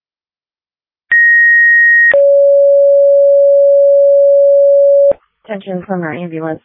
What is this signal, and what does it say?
Two-tone page, heard as a one-second tone, then a three-second tone, about 1900 then 560 Hz, with speech. It is heard from 1.1 to 5.1 s.